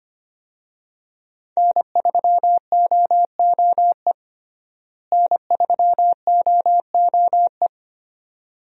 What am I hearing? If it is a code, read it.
Morse code, heard as a keyed tone, about 25 wpm, 700 Hz. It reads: N3OOE N3OOE
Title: USA Call Signs - 5 Characters - Repeated 2x - 25wpm